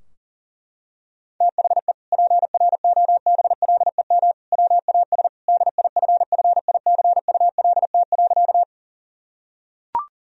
Morse code, heard as a keyed tone, about 40 wpm, 700 Hz. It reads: THE PROBLEM WAS DIFFICULT.
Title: Sentences from Top 1000 Words - Review Part 2 - 40wpm